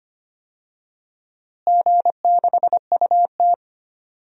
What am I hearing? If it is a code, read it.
Morse code, heard as a keyed tone, about 25 wpm, 700 Hz. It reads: G6UT
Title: DXCC-100 Call Signs - 4 Characters - Repeated 3x - 25wpm